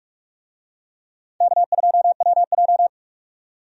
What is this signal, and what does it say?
Morse code, heard as a keyed tone, about 43 wpm, 700 Hz. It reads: K2WJ
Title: USA Call Signs - 4 Characters - 45wpm